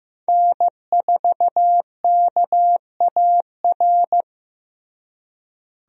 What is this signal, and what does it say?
Morse code, heard as a keyed tone, about 15 wpm, 700 Hz. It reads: N4KAR
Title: USA Call Signs - 5 Characters - Repeated 2x - 15wpm